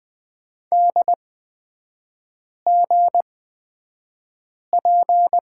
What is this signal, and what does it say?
Morse code, heard as a keyed tone, about 20 wpm, spaced as for 5 wpm, 690 Hz. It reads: DGP
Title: 229 Lesson 37e - Letter x - Three Characters 20wpm